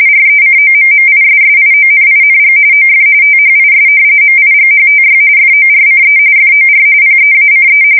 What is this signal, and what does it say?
RTTY, about 45.45 baud. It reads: RECOMMISSIONING OF THIS MIGHTY FORCE FOR PEACE
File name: fsk170 / battleshipnj